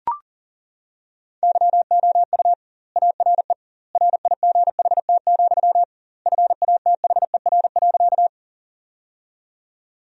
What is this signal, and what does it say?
Morse code, heard as a keyed tone, about 40 wpm, 700 Hz. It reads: YOU ARE RIGHT, FATHER.
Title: 2x Word-Spacing - Sentences from Top 300 Words - 40wpm